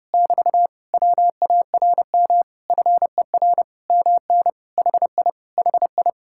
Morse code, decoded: <BT> WARM FER MN HI HI

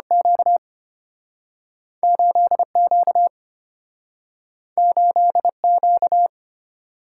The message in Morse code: Q 8Q 8Q

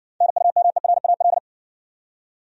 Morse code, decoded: NUMERAL

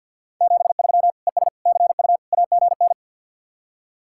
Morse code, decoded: 73 ES CU AGN

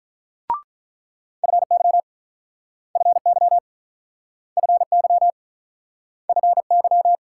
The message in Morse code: FY FY FY FY

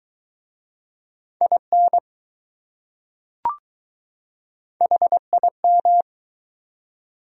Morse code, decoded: IN HIM

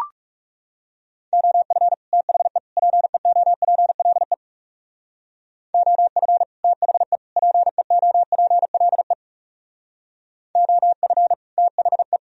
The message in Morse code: OF THE PEOPLE OF THE PEOPLE OF THE

